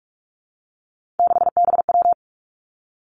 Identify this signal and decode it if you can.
Morse code, heard as a keyed tone, about 46 wpm, 700 Hz. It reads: 6BW
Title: Three Challenging Letter-Number Combinations 45wpm